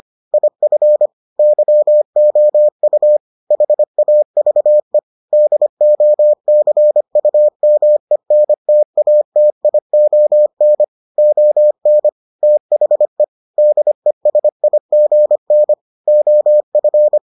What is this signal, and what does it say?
Morse code, heard as a keyed tone, about 25 wpm, 600 Hz. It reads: IF YOU HAVE DOCUMENTATION ON THE DESIGN OF